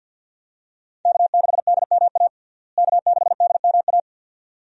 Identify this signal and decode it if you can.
Morse code, heard as a keyed tone, about 48 wpm, 700 Hz. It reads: K6DMA K6DMA
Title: USA Call Signs - 5 Characters - Repeated 2x - 50wpm